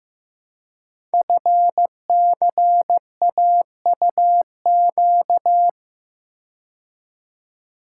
Morse code, heard as a keyed tone, about 15 wpm, 690 Hz. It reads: FCAUQ
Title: Code Groups - 5 Letters - 15wpm